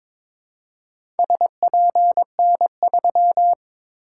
Morse code, heard as a keyed tone, about 22 wpm, 700 Hz. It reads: SPN3
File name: practice / 22wpm / Four Letter-Number Combinations - 22wpm